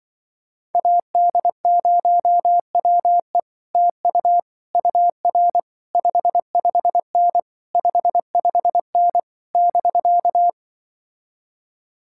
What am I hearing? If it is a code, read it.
Morse code, heard as a keyed tone, about 24 wpm, 700 Hz. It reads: AD0WE TU UR 55N 55N <BK>
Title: Speed-Racing Common QSO Phrases - 20wpm